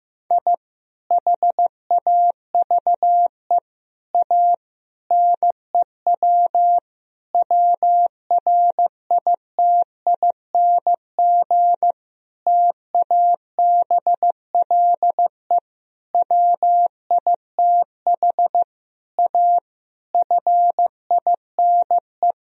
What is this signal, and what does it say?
Morse code, heard as a keyed tone, about 15 wpm, 700 Hz. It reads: I HAVE A NEW WRITING TABLE WITH A FINE
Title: Sentences from Top 1000 Words - Review Part 3 - 15wpm